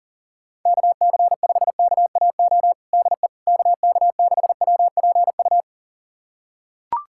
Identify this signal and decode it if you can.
Morse code, heard as a keyed tone, about 40 wpm, 700 Hz. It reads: KC5KAO DE KK6WPU
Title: USA Callsign DE Callsign 40wpm